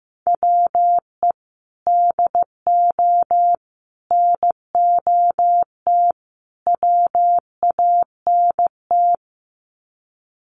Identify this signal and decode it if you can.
Morse code, heard as a keyed tone, about 15 wpm, 700 Hz. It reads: WE DO NOT WANT
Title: Sets of 4 Words 15wpm